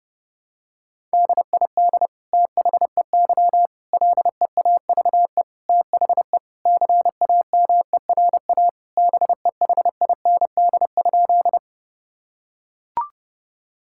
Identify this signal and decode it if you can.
Morse code, heard as a keyed tone, about 30 wpm, 700 Hz. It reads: DID THEY LEAVE THE CAMERA BEHIND?
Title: Sentences from Top 1000 Words - Review Part 5 - 30wpm